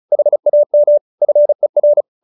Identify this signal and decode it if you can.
Morse code, heard as a keyed tone, about 34 wpm, 590 Hz. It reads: HAM FER